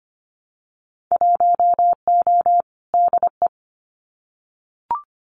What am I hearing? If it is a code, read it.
Morse code, heard as a keyed tone, about 25 wpm, 700 Hz. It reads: 1O DE
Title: Winter Field Day Exchanges 25wpm